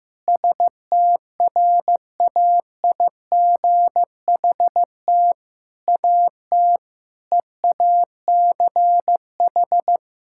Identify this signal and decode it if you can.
Morse code, heard as a keyed tone, about 15 wpm, 700 Hz. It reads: STRAIGHT AT EACH
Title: Sentences from Top 1000 Words - Review Part 6 - 15wpm